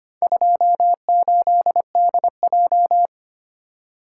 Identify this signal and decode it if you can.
Morse code, heard as a keyed tone, about 25 wpm, 700 Hz. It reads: 28DJ
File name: practice / 25wpm / Four Letter-Number Combinations - 25wpm